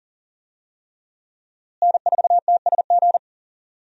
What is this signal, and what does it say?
Morse code, heard as a keyed tone, about 40 wpm, 700 Hz. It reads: N4TSG
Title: USA Call Signs - 5 Characters - Repeated 3x - 40wpm